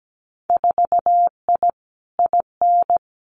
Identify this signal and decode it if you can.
Morse code, heard as a keyed tone, about 17 wpm, 700 Hz. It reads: 4I IN
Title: Winter Field Day Exchanges 17wpm